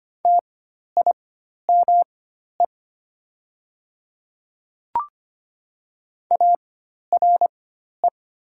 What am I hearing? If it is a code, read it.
Morse code, heard as a keyed tone, about 25 wpm, spaced as for 12 wpm, 700 Hz. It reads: TIME ARE
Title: 4x Character Spacing - Top 300 Words - 25wpm